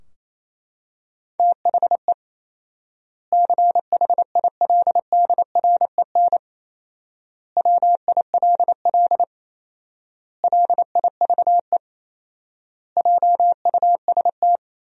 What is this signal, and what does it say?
Morse code, heard as a keyed tone, about 28 wpm, 700 Hz. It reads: THE CHILDREN WILL LIVE JUST
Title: 4x Word-Spacing - Sentences from Top 300 Words - 28wpm